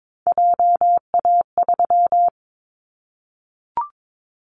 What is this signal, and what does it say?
Morse code, heard as a keyed tone, about 22 wpm, 700 Hz. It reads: JA3
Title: Callsign Prefixes for Top 25 DXCC - 22wpm